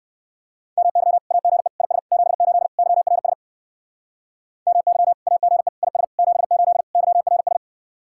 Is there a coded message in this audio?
Yes: Morse code from 0.8 to 7.6 s, heard as a keyed tone, about 59 wpm, 700 Hz. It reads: MY AGE IS 78 YRS MY AGE IS 78 YRS